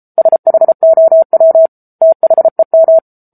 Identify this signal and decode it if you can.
Morse code, heard as a keyed tone, about 34 wpm, 650 Hz. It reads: SHOW THEM
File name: Book Two / 07-Chapter 7 - 33 WPM